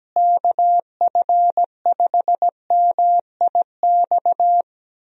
Morse code, heard as a keyed tone, about 17 wpm, 700 Hz. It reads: KF5MIX
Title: USA Call Signs - 6 Characters - Repeated 3x - 17wpm